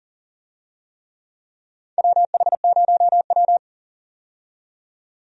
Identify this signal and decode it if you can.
Morse code, heard as a keyed tone, about 40 wpm, 700 Hz. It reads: WH0W
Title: USA Call Signs - 4 Characters - 40wpm